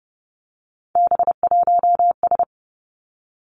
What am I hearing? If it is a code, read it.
Morse code, heard as a keyed tone, about 30 wpm, 700 Hz. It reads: B1S
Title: Three Challenging Letter-Number Combinations 30wpm